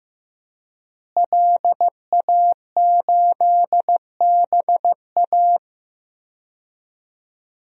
Morse code, heard as a keyed tone, about 15 wpm, 700 Hz. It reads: LA8BA
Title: DXCC-100 Call Signs - 5 Characters - Repeated 2x - 15wpm